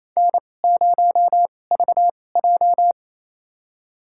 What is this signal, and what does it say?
Morse code, heard as a keyed tone, about 28 wpm, spaced as for 20 wpm, 700 Hz. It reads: N0VJ
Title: USA Call Signs - 4 Characters - 2x Character Spacing - 28wpm